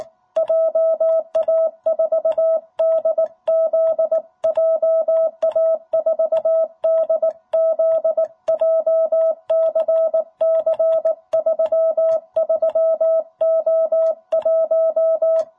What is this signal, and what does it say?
Morse code, heard as a keyed tone, about 18 wpm, 660 Hz. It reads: JA4DZ JA4DZ JCC 33O1
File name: jh8cbh / 2021cw